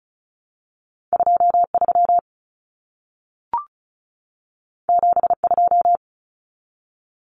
Morse code, heard as a keyed tone, about 36 wpm, 700 Hz. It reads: 23 72
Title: Common Cut Numbers - 2-Digit 35wpm